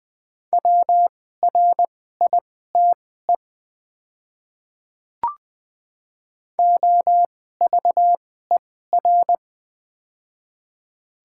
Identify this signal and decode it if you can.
Morse code, heard as a keyed tone, about 20 wpm, spaced as for 14 wpm, 700 Hz. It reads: WRITE OVER
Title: 2x Character Spacing - Top 300 Words - 20wpm